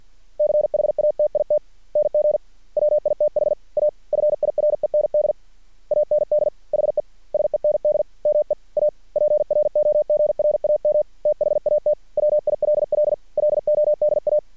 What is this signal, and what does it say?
Morse code, heard as a keyed tone, about 50 wpm, 600 Hz. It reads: CHATITNG WITH A FIREND AND HE SEND ME A PROGRAM THAT WILL LODA
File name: archived / CRSnet